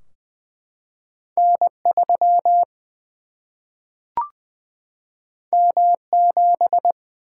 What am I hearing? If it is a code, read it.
Morse code, heard as a keyed tone, about 20 wpm, 700 Hz. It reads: N 3 M 7
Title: Single-Letter Single-Number 20wpm